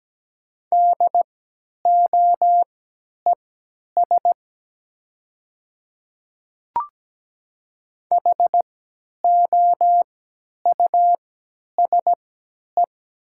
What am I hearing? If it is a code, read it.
Morse code, heard as a keyed tone, about 17 wpm, spaced as for 10 wpm, 700 Hz. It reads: DOES HOUSE
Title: 3x Character Spacing - Top 500 Words - 17wpm